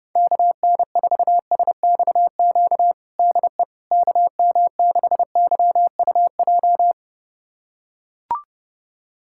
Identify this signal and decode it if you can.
Morse code, heard as a keyed tone, about 32 wpm, 700 Hz. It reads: KN4SXQ DE KM6YUJ
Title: USA Callsign DE Callsign 30wpm